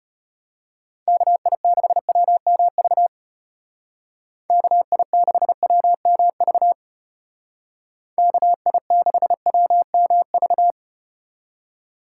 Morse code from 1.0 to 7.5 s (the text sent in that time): KI6WMV KI6WMV